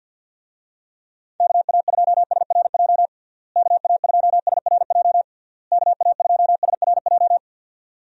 Morse code, heard as a keyed tone, about 50 wpm, 700 Hz. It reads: KA2SRJ KA2SRJ KA2SRJ
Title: USA Call Signs - 6 Characters - Repeated 3x - 50wpm